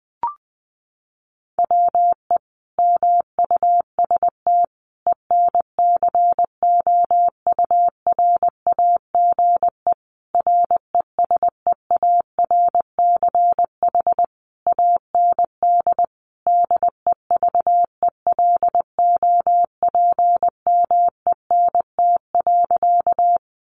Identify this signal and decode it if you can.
Morse code, heard as a keyed tone, about 20 wpm, 700 Hz. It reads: WE MUST ENCOURAGE RESEARCH AND DEVELOPMENT.
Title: Sentences from Top 1400 Words 20wpm